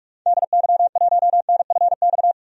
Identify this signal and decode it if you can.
Morse code, heard as a keyed tone, about 46 wpm, 700 Hz. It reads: DY1NFX
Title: DXCC-100 Call Signs - 6 Characters - 45wpm